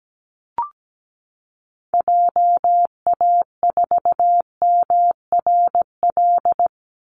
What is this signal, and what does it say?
Morse code, heard as a keyed tone, about 17 wpm, 700 Hz. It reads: JA4MRL
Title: Call Signs Repeated 3x - 17wpm